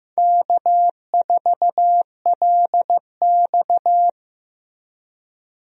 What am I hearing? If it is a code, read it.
Morse code, heard as a keyed tone, about 15 wpm, 700 Hz. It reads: K4LX
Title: USA Call Signs - 4 Characters - Repeated 3x - 15wpm